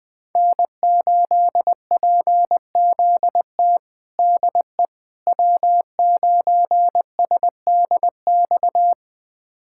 Morse code, heard as a keyed tone, about 20 wpm, 700 Hz. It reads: N8PZT DE W9SDX
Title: USA Callsign DE Callsign - 5 Characters - 20wpm